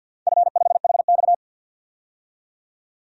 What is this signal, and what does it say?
Morse code, heard as a keyed tone, about 50 wpm, 700 Hz. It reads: F5HX
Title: DXCC-100 Call Signs - 4 Characters - 50wpm